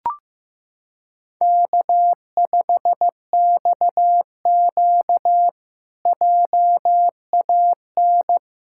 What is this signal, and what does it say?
Morse code, heard as a keyed tone, about 15 wpm, 700 Hz. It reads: K5XQ JAN
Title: K1USN SST QSO Practice - 15wpm